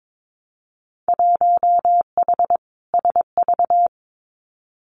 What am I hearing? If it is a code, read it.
Morse code, heard as a keyed tone, about 22 wpm, 700 Hz. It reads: 1H SV
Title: Winter Field Day Exchanges 22wpm